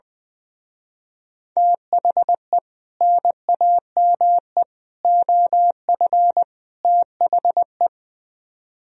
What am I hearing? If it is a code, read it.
Morse code, heard as a keyed tone, about 20 wpm, 700 Hz. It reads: THE NAME OF THE
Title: Speed-Racing - Sets of 4 Words 20wpm